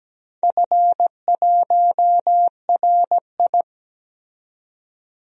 Morse code, heard as a keyed tone, about 17 wpm, 700 Hz. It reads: F1RI
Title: DXCC-100 Call Signs - 4 Characters - 17wpm